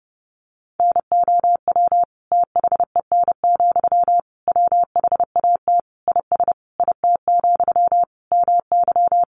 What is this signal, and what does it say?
Morse code, heard as a keyed tone, about 30 wpm, 700 Hz. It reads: NOW THEN, WHAT IS IT, MY
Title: Sentences from Top 1100 Words 30wpm